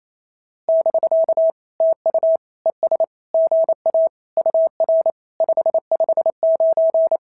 Morse code, heard as a keyed tone, about 27 wpm, 650 Hz. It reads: <BK> TU ES GA UR 559
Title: SOTA QSO Practice 28wpm